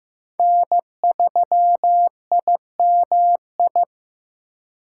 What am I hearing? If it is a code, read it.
Morse code, heard as a keyed tone, about 15 wpm, 700 Hz. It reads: N3IMI
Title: USA Call Signs - 5 Characters - 2x Character Spacing - 15wpm